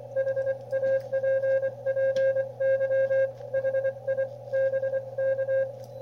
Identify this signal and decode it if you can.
Morse code, heard as a keyed tone, about 24 wpm, 530 Hz. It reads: HAPPY HI BK
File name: qrp / wb0jnr at 100mW with ae5x 02-23-2018